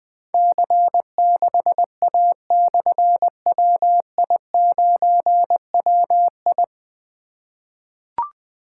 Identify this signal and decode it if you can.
Morse code, heard as a keyed tone, about 21 wpm, 700 Hz. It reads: C6A/WI9WI
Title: Difficult Call Signs 20wpm